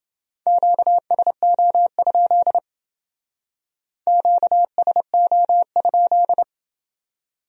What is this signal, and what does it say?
Morse code, heard as a keyed tone, about 30 wpm, 700 Hz. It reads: QSO? QSO?